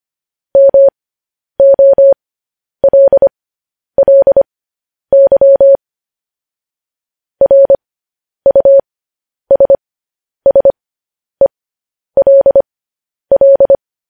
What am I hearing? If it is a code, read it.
Morse code, heard as a keyed tone, about 25 wpm, spaced as for 10 wpm, 550 Hz. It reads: MOLLY RUSSELL
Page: news headlines in morse code at 10 wpm